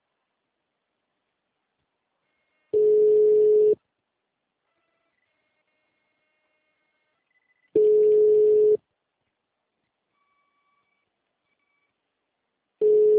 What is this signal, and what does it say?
Ringback tone (Europe, 425 Hz).